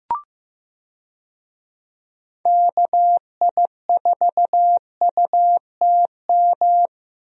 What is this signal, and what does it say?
Morse code, heard as a keyed tone, about 15 wpm, 700 Hz. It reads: KI4UTM
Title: USA Call Signs - 6 Characters - Repeated 3x - 15wpm